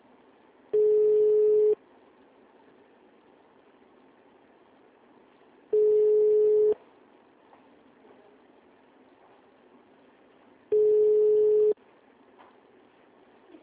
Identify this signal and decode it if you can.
Ringback tone (Europe, 425 Hz), with speech.
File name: out- -2003- - - .39089